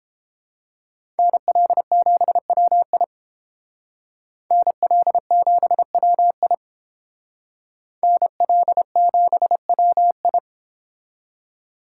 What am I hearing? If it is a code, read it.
Morse code, heard as a keyed tone, about 30 wpm, 700 Hz. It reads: NL7WI NL7WI NL7WI